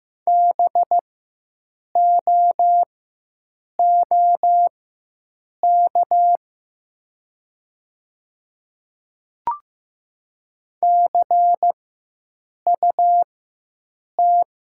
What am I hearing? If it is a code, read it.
Morse code, heard as a keyed tone, about 15 wpm, spaced as for 7 wpm, 700 Hz. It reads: BOOK CUT
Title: 4x Character Spacing - Top 500 Words - 15wpm